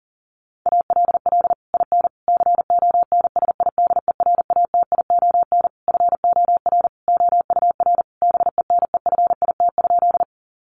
Morse code, heard as a keyed tone, about 40 wpm, 700 Hz. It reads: ALL IN CONSIDERATION FOR OUR BENEFIT?
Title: Sentences from Top 1500 Words 40wpm